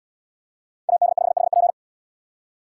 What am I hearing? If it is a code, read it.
Morse code, heard as a keyed tone, about 73 wpm, 700 Hz. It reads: AD5HP